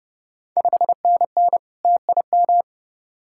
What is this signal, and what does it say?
Morse code, heard as a keyed tone, about 30 wpm, 700 Hz. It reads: 5NN TIM
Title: Ontario QSO Party Exchange 30wpm